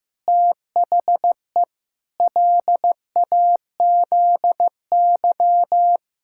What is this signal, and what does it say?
Morse code, heard as a keyed tone, about 15 wpm, 700 Hz. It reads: THE LAZY
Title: Pangram Challenge v2 - 15wpm